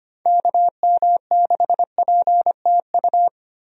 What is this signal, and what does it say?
Morse code, heard as a keyed tone, about 25 wpm, 700 Hz. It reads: KM6PTU
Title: USA Call Signs - 6 Characters - 25wpm